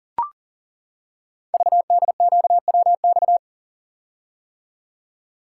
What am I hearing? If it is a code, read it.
Morse code, heard as a keyed tone, about 41 wpm, 700 Hz. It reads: VDQWX
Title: Code Groups - 5 Letter:Numbers - 40wpm